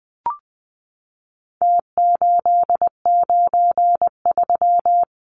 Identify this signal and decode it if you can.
Morse code, heard as a keyed tone, about 20 wpm, 700 Hz. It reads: T893
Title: Numbers - Contest Style - 4-Digit 20wpm